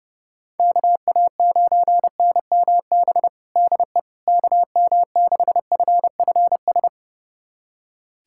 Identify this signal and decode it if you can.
Morse code, heard as a keyed tone, about 32 wpm, 700 Hz. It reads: KA9NMB DE KM6FFS
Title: USA Callsign DE Callsign 30wpm